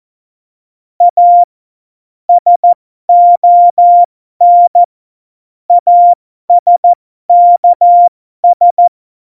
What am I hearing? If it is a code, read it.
Morse code, heard as a keyed tone, about 14 wpm, spaced as for 12 wpm, 700 Hz. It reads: A SON ASKS
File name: Morsekurs / Witze